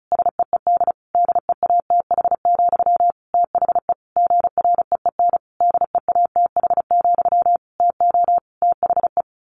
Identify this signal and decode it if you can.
Morse code, heard as a keyed tone, about 34 wpm, 700 Hz. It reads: SEED DEATH, THE GREEN DEATH, TO THE
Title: A Sound Of Thunder 35wpm